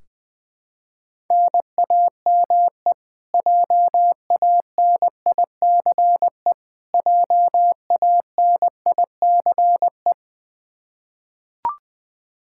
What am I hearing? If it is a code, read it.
Morse code, heard as a keyed tone, about 20 wpm, 700 Hz. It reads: NAME JANICE JANICE
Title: Practice QSO - Exchange 1 - Beginning - 20wpm